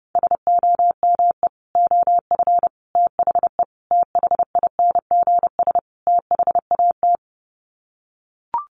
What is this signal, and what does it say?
Morse code, heard as a keyed tone, about 29 wpm, 700 Hz. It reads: SOME OF THE THINGS THAT
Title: Sets of 5 Words 30wpm